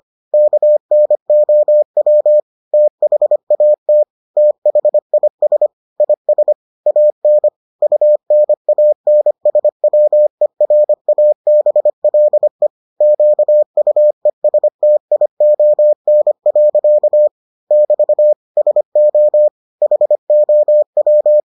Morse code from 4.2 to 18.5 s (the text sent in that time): THIS IS AN UNANSWERABLE QUESTION. <BT>